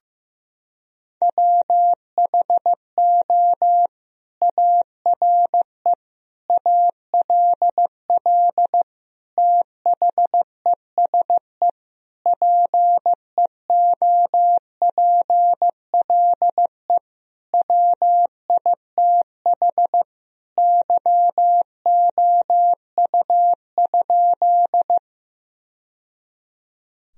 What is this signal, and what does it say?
Morse code, heard as a keyed tone, about 15 wpm, 700 Hz. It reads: WHO ARE ALL THESE PEOPLE WITH YOU?
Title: Sentences from Top 1000 Words - Review Part 1 - 15wpm